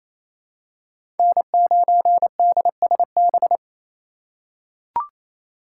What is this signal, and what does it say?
Morse code, heard as a keyed tone, about 28 wpm, 700 Hz. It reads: N9DSB
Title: USA Call Signs - 5 Characters - 2x Character Spacing - 28wpm